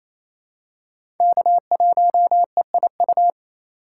Morse code, heard as a keyed tone, about 28 wpm, 700 Hz. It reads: K1EIU
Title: USA Call Signs - 5 Characters - 2x Character Spacing - 28wpm